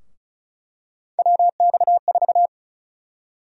Morse code, heard as a keyed tone, about 34 wpm, 700 Hz. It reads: WX4
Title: USA Call Sign Prefixes - Repeated 2x - 35wpm